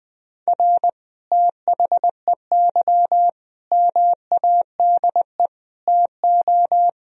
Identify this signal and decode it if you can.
Morse code, heard as a keyed tone, about 20 wpm, 700 Hz. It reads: R THEY MADE TO
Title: Sentences from Top 400 Words and Abbreviations 20wpm